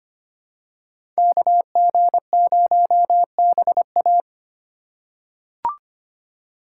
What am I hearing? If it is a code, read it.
Morse code, heard as a keyed tone, about 25 wpm, 700 Hz. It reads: KG0BA
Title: USA Call Signs - 5 Characters - 4x Character Spacing - 25wpm